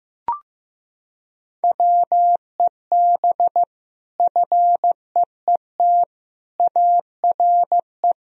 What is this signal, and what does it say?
Morse code, heard as a keyed tone, about 15 wpm, 700 Hz. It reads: WEB FEET ARE
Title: Sentences from Top 1000 Words - Review Part 4 - 15wpm